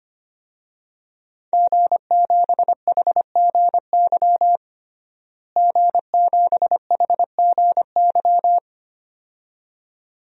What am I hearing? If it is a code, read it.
Morse code, heard as a keyed tone, about 25 wpm, 700 Hz. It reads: G7HGY G7HGY
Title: DXCC-100 Call Signs - 5 Characters - Repeated 2x - 25wpm